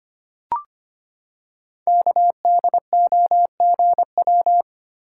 Morse code, heard as a keyed tone, about 25 wpm, 700 Hz. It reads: KDOGW
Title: Code Groups - 5 Letters - 25wpm